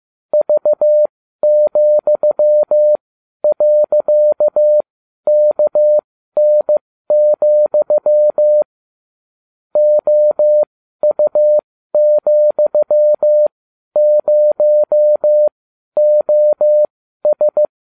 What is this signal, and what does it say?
Morse code, heard as a keyed tone, about 15 wpm, spaced as for 12 wpm, 600 Hz. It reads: V,.KN, OU,0OS
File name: cw / ubacw20 3